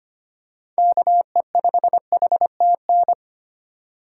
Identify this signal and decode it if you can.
Morse code, heard as a keyed tone, about 25 wpm, 700 Hz. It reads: KE5HTN
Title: USA Call Signs - 6 Characters - 4x Character Spacing - 25wpm